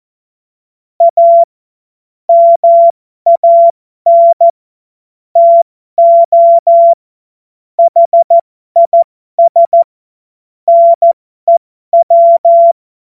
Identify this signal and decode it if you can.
Morse code, heard as a keyed tone, about 14 wpm, spaced as for 12 wpm, 680 Hz. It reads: A MAN TO HIS NEW
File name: Morsekurs / Witze